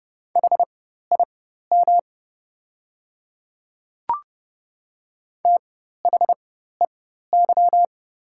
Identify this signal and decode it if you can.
Morse code, heard as a keyed tone, about 30 wpm, spaced as for 14 wpm, 700 Hz. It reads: HIM THEY
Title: 4x Character Spacing - Top 100 Words - 30wpm